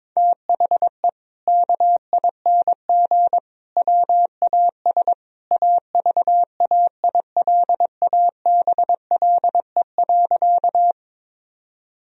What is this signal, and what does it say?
Morse code, heard as a keyed tone, about 22 wpm, 700 Hz. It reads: THE KING WAS AVAILABLE.
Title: Sentences from Top 1000 Words - Review Part 3 - 22wpm